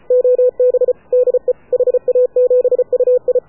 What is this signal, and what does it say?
Morse code, heard as a keyed tone, about 33 wpm, 500 Hz. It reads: OB DE HA7UI
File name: AUDIO2019 / HA7UI CW